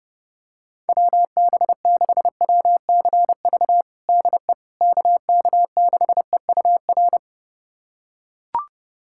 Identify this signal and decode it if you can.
Morse code, heard as a keyed tone, about 30 wpm, 700 Hz. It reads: WB6WCV DE KK6EUR
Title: USA Callsign DE Callsign - 6 Characters - 30wpm